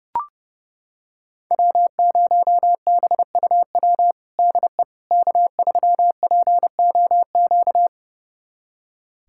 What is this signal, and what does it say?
Morse code, heard as a keyed tone, about 30 wpm, 700 Hz. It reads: W0BUW DE K3POQ
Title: USA Callsign DE Callsign - 5 Characters - 30wpm